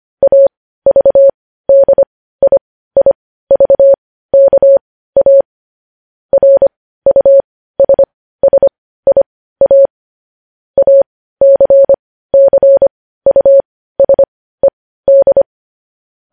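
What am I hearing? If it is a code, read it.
Morse code, heard as a keyed tone, about 25 wpm, spaced as for 15 wpm, 550 Hz. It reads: AVDIIVKA RUSSIA ACCUSED
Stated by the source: Morse code news headlines.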